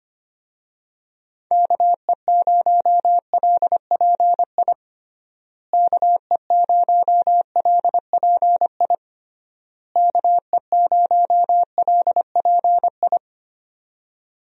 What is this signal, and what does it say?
Morse code, heard as a keyed tone, about 25 wpm, 700 Hz. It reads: KE0LPI KE0LPI KE0LPI
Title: USA Call Signs - 6 Characters - Repeated 3x - 25wpm